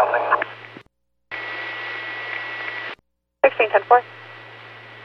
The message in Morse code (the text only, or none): none